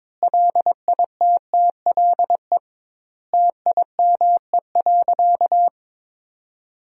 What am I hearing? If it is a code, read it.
Morse code, heard as a keyed tone, about 22 wpm, 700 Hz. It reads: LITTLE TIME.
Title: 2x Word-Spacing - Sentences from Top 200 Words - 22wpm